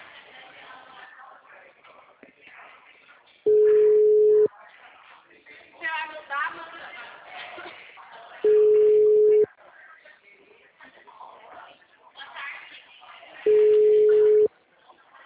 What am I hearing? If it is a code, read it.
Ringback tone (Europe, 425 Hz), with speech.